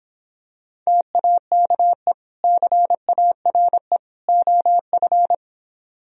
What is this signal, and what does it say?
Morse code, heard as a keyed tone, about 26 wpm, 700 Hz. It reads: TAKE CARE OF